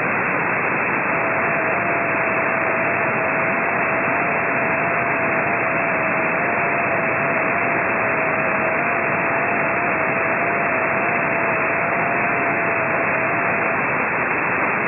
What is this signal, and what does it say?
FT8.
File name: ft8-vs-cw / 55-to-150 FT8 -10